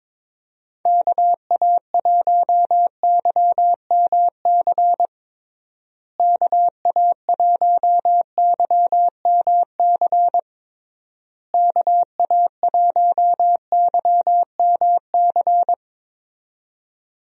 Morse code, heard as a keyed tone, about 22 wpm, 700 Hz. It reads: KA1YMC KA1YMC KA1YMC
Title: USA Call Signs - 6 Characters - Repeated 3x - 22wpm